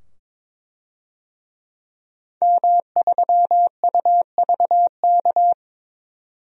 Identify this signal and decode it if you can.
Morse code, heard as a keyed tone, about 22 wpm, 700 Hz. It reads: M3UVK
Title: DXCC-100 Call Signs - 5 Characters - 22wpm